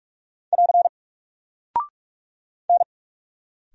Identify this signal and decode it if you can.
Morse code, heard as a keyed tone, about 44 wpm, 700 Hz. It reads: <AR> N
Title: Single Character (Letter:Number:Punctuation:Prosign) Alt 45wpm